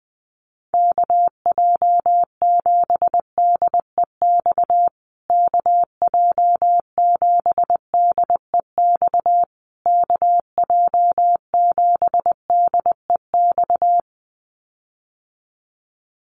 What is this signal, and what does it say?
Morse code, heard as a keyed tone, about 20 wpm, 700 Hz. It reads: KJ7DEX KJ7DEX KJ7DEX
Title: USA Call Signs Repeated 3x 20wpm